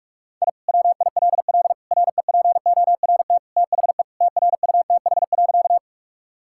Morse code, heard as a keyed tone, about 41 wpm, 700 Hz. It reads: I WILL REPORT THE TRUTH.
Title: Sentences from Top 1000 Words - Review Part 6 - 45wpm